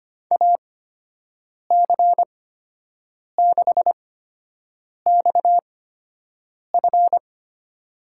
Morse code, heard as a keyed tone, about 25 wpm, spaced as for 7 wpm, 700 Hz. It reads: AC6XF
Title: USA Call Signs - 5 Characters - 8x Character Spacing - 25wpm